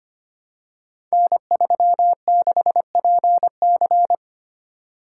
Morse code, heard as a keyed tone, about 25 wpm, 700 Hz. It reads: N36PC
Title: Code Groups - 5 Letter:Numbers - 25wpm